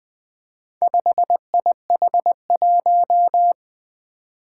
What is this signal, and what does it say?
Morse code, heard as a keyed tone, about 20 wpm, 700 Hz. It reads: 5IH1
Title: Four Challenging Letter-Number Combinations 20wpm